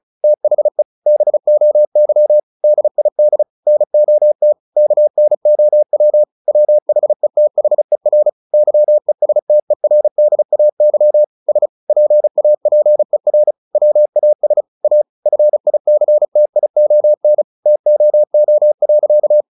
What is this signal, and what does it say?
Morse code, heard as a keyed tone, about 34 wpm, 600 Hz. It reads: THE BOY DID NOT KNOW WHETHER YESTERDAY S PAPER WAS A FICTION TOO.
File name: the old man and the sea / the old man and the sea 1 35wpm